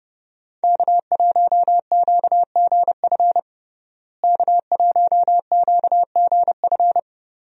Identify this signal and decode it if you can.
Morse code, heard as a keyed tone, about 30 wpm, 700 Hz. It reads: K1QGF K1QGF
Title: USA Call Signs - 5 Characters - Repeated 3x - 30wpm